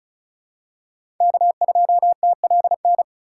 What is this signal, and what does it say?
Morse code, heard as a keyed tone, about 36 wpm, 700 Hz. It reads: K2TLN
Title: USA Call Signs - 5 Characters - 2x Character Spacing - 35wpm